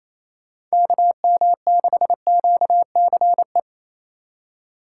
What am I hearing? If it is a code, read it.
Morse code, heard as a keyed tone, about 34 wpm, 700 Hz. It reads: KM6QCE
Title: Speed Racing - USA 2x3 Call Signs 28wpm